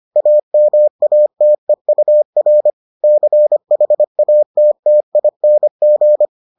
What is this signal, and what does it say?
Morse code, heard as a keyed tone, about 25 wpm, 600 Hz. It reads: AMATEUR CHATTING